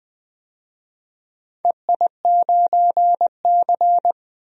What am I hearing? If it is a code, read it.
Morse code, heard as a keyed tone, about 20 wpm, 700 Hz. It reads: EI9C
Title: DXCC-100 Call Signs - 4 Characters - 20wpm